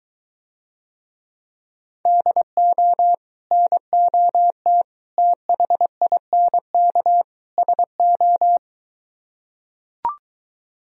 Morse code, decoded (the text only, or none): DO NOT THINK SO